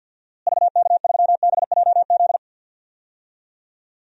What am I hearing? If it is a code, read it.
Morse code, heard as a keyed tone, about 50 wpm, 700 Hz. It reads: VK3BJZ
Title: DXCC-100 Call Signs - 6 Characters - Repeated 3x - 50wpm